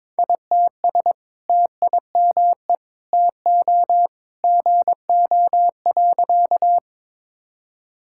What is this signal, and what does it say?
Morse code, heard as a keyed tone, about 22 wpm, 700 Hz. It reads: ITS TIME TO GO.
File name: practice / 22wpm / Sentences with 2-Word Phrases - 22wpm